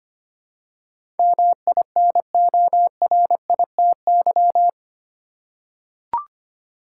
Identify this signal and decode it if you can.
Morse code, heard as a keyed tone, about 25 wpm, 700 Hz. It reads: MINORITY
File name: Eight Letter Words - 25wpm